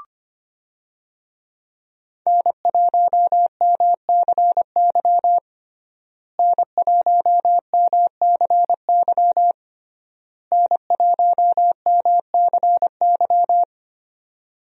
Morse code, heard as a keyed tone, about 25 wpm, 700 Hz. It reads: N1MCY N1MCY N1MCY
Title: USA Call Signs - 5 Characters - Repeated 3x - 25wpm